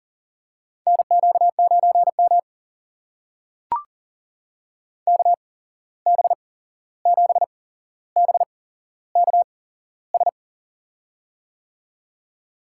Morse code, heard as a keyed tone, about 40 wpm, spaced as for 15 wpm, 700 Hz. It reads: NQ9M KB7BKS
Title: USA Call Signs - 8x Character Spacing - 40wpm